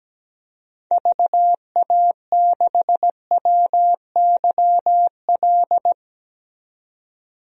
Morse code, heard as a keyed tone, about 17 wpm, 700 Hz. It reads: VA6WYL
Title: DXCC-100 Call Signs - 6 Characters - 17wpm